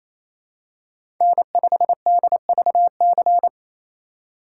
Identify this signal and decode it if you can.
Morse code, heard as a keyed tone, about 28 wpm, 700 Hz. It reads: N5DVC
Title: USA Call Signs - 5 Characters - 2x Character Spacing - 28wpm